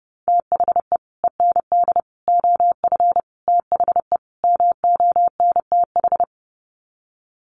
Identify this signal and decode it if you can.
Morse code, heard as a keyed tone, about 30 wpm, 700 Hz. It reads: THE END OF THE MONTH
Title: Sets of 5 Words Encore 30wpm